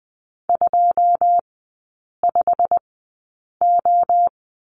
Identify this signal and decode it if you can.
Morse code, heard as a keyed tone, about 20 wpm, spaced as for 8 wpm, 700 Hz. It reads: 25O
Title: 178 Lesson 29e - Slash - Three Characters 20wpm